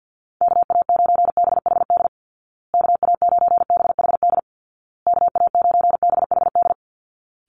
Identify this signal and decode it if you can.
Morse code, heard as a keyed tone, about 50 wpm, 700 Hz. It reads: KA9BHD KA9BHD KA9BHD
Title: Call Signs Repeated 2x - 50wpm